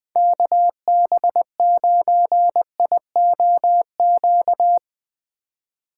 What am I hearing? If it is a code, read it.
Morse code, heard as a keyed tone, about 22 wpm, 700 Hz. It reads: KB9IOQ
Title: Speed Racing - USA 2x3 Call Signs 20wpm